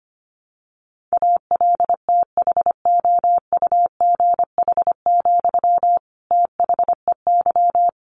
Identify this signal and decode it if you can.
Morse code, heard as a keyed tone, about 25 wpm, 700 Hz. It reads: ALTHOUGH, THEY
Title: Sentences from Top 600 Words 25wpm